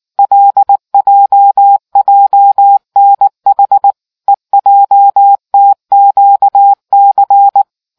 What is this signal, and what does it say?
Morse code, heard as a keyed tone, about 19 wpm, 800 Hz. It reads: LJJNH EJTQC